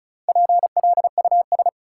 Morse code, heard as a keyed tone, about 36 wpm, 700 Hz. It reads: PLUS
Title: Four Letter Words - 35wpm